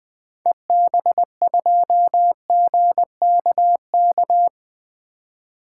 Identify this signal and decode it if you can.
Morse code, heard as a keyed tone, about 21 wpm, 700 Hz. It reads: EB2GKK
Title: 2023 SOTA Activator Call Signs - 20wpm